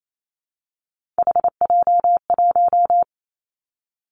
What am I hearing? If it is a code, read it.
Morse code, heard as a keyed tone, about 28 wpm, 700 Hz. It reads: HJ1
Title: Three Challenging Letter-Number Combinations 28wpm